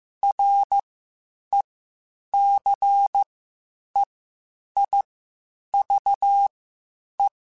Morse code, heard as a keyed tone, about 15 wpm, spaced as for 8 wpm, 800 Hz. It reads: RECEIVE
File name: Quotes Spaced / Quotes Spaced-mp3s / Advice sp @13